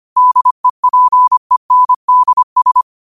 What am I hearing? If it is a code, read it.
Morse code, heard as a keyed tone, about 25 wpm, 1000 Hz. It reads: DEPENDS
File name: qotdfiles / 25wpm